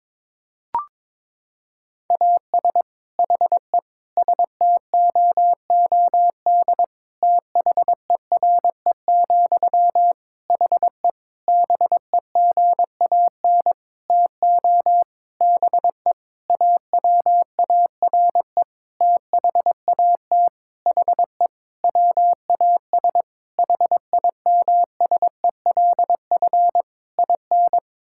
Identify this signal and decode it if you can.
Morse code, heard as a keyed tone, about 22 wpm, 700 Hz. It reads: AS HE STOOD THERE, HE BEGAN TO BE AWARE THAT HE WAS HIMSELF IN
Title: Sentences from Top 1600 Words 22wpm